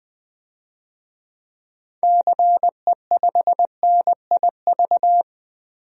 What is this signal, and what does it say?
Morse code, heard as a keyed tone, about 20 wpm, 700 Hz. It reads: CE5NIV
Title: DXCC-100 Call Signs - 6 Characters - Repeated 3x - 20wpm